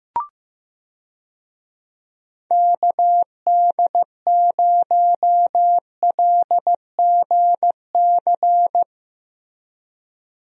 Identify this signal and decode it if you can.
Morse code, heard as a keyed tone, about 15 wpm, 700 Hz. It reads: KD0LGC
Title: USA Call Signs - 6 Characters - Repeated 2x - 15wpm